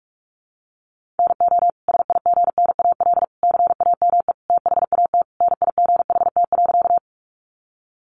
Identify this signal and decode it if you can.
Morse code, heard as a keyed tone, about 44 wpm, 700 Hz. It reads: NO SIGNAL CAME THAT NIGHT.
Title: Sentences from Top 1400 Words 45wpm